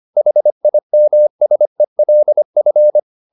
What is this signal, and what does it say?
Morse code, heard as a keyed tone, about 25 wpm, 600 Hz. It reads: HIMSELF